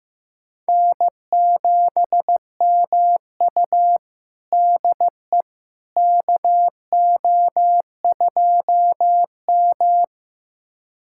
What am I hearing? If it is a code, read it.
Morse code, heard as a keyed tone, about 15 wpm, 700 Hz. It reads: N7MU DE KO2M
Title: USA Callsign DE Callsign - 4 Characters - 15wpm